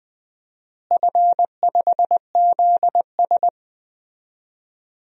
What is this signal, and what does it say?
Morse code, heard as a keyed tone, about 21 wpm, 700 Hz. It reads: F5ZS
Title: DXCC-100 Call Signs - 4 Characters - Repeated 2x - 20wpm